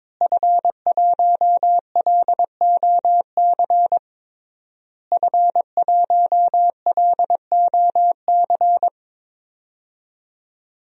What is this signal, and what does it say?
Morse code, heard as a keyed tone, about 22 wpm, 700 Hz. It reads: F1LOC F1LOC
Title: DXCC-100 Call Signs - 5 Characters - Repeated 3x - 22wpm